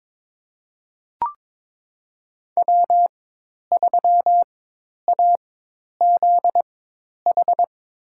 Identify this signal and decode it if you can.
Morse code, heard as a keyed tone, about 22 wpm, spaced as for 10 wpm, 700 Hz. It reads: W3AZH
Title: USA Call Signs - 5 Characters - 4x Character Spacing - 22wpm